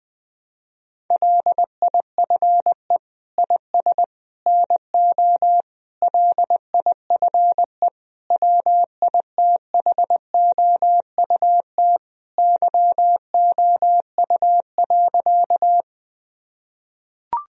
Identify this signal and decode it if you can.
Morse code, heard as a keyed tone, about 20 wpm, 700 Hz. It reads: LIFE IS NO LIFE WITHOUT YOU.
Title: Sentences from Top 1000 Words - Review Part 3 - 20wpm